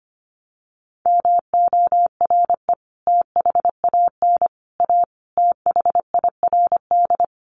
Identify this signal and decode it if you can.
Morse code, heard as a keyed tone, about 25 wpm, 700 Hz. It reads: MORE THAN A THIRD
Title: Sets of 5 Words Encore2 25wpm